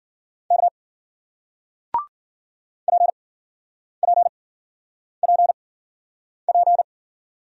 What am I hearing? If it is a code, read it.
Morse code, heard as a keyed tone, about 50 wpm, spaced as for 9 wpm, 700 Hz. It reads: K PPPP
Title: Speed-Racing Single Letters 40wpm